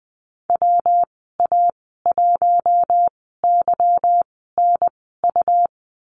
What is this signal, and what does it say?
Morse code, heard as a keyed tone, about 20 wpm, spaced as for 15 wpm, 700 Hz. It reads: WA1YNU
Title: 179 Lesson 29g - Slash - Call Signs 20wpm